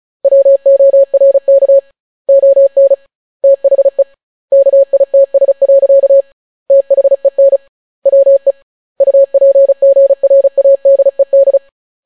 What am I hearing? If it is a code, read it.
Morse code, heard as a keyed tone, about 36 wpm, 550 Hz. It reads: WORK ON THE KITS. THEN WE UPGRADED